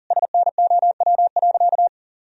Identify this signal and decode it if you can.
Morse code, heard as a keyed tone, about 40 wpm, 700 Hz. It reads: SNOW.
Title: Sentences from Top 1000 Words - Review Part 3 - 40wpm